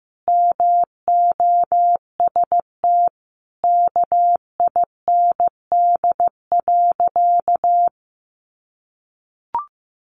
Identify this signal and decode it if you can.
Morse code, heard as a keyed tone, about 15 wpm, 700 Hz. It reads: MOST KIND.
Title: Sentences from Top 200 Words 15wpm